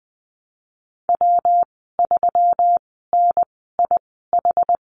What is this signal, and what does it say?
Morse code, heard as a keyed tone, about 20 wpm, spaced as for 14 wpm, 700 Hz. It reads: W3NIH
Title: 122 Lesson 20g - Number 3 - Call Signs 20wpm